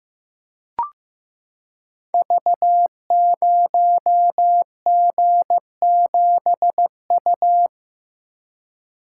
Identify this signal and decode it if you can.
Morse code, heard as a keyed tone, about 15 wpm, 700 Hz. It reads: V0G7U
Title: Code Groups - 5 Letter:Numbers - 15wpm